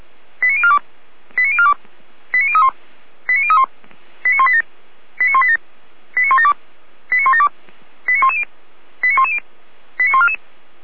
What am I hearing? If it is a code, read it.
Five-tone selective call (ZVEI), heard as quick runs of five tones, tones of about 70 ms. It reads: 89042 89042 89031 89031 89187 89187 89172 89172 89110 89110 89144